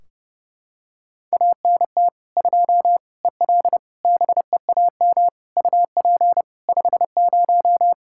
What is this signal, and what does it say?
Morse code, heard as a keyed tone, about 32 wpm, 700 Hz. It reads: ANT 2 EL BEAM UP 50
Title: Practice QSO - Exchange 2 - Info - 30wpm